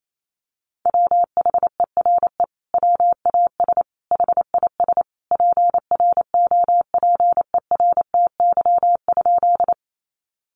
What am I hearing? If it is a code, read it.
Morse code, heard as a keyed tone, about 28 wpm, 700 Hz. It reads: WHERE WAS HIS PROPERTY?
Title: Sentences from Top 1000 Words 28wpm